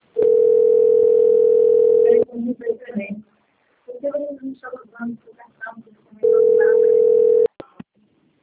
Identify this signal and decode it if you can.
Ringback tone (North America), 440 + 480 Hz, with speech.
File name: rg-1600-unknown- - - .18568